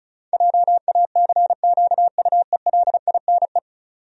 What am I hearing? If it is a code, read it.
Morse code, heard as a keyed tone, about 34 wpm, 700 Hz. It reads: JACQUELINE
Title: USA Common Names - 35wpm